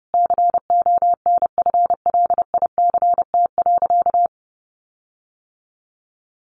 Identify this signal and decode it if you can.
Morse code, heard as a keyed tone, about 30 wpm, 700 Hz. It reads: CONFLICT.
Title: Sentences from Top 1300 Words 30wpm